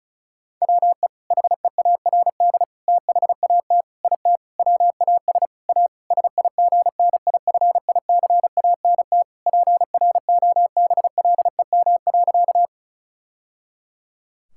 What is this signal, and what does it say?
Morse code, heard as a keyed tone, about 36 wpm, 700 Hz. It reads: WE HEARD THAT IT WAS A SIGNIFICANT PROBLEM.
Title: Sentences from Top 1000 Words - Review Part 6 - 35wpm